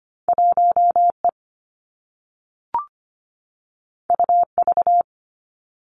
Morse code, heard as a keyed tone, about 25 wpm, 700 Hz. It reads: 1E UV
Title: Two Challenging Letter-Number Combinations 25wpm